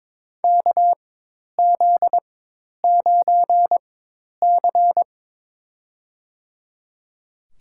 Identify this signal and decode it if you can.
Morse code, heard as a keyed tone, about 22 wpm, spaced as for 10 wpm, 700 Hz. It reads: KZ9C
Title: USA Call Signs - 4 Characters - 4x Character Spacing - 22wpm